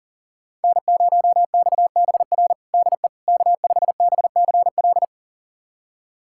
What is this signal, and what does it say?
Morse code, heard as a keyed tone, about 37 wpm, 700 Hz. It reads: N0XBR DE K5BCL
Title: USA Callsign DE Callsign - 5 Characters - 40wpm